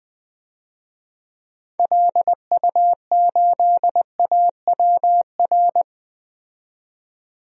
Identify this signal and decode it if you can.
Morse code, heard as a keyed tone, about 20 wpm, 700 Hz. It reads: LU8AWR
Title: DXCC-100 Call Signs - 6 Characters - 20wpm